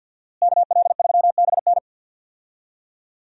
Morse code, heard as a keyed tone, about 50 wpm, 700 Hz. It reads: KL3BN
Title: USA Call Signs - 5 Characters - Repeated 3x - 50wpm